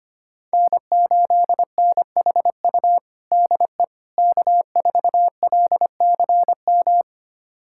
Morse code, heard as a keyed tone, about 25 wpm, 700 Hz. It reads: N8NHU DE K4LCM
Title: USA Callsign DE Callsign - 5 Characters - 25wpm